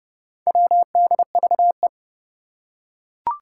Morse code, heard as a keyed tone, about 30 wpm, 690 Hz. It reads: WDVE
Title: Four Challenging Letter-Number Combinations 30wpm